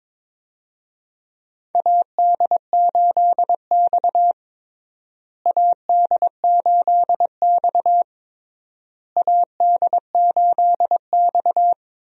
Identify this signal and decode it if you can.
Morse code, heard as a keyed tone, about 22 wpm, 700 Hz. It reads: AD8X AD8X AD8X
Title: USA Call Signs - 4 Characters - Repeated 3x - 22wpm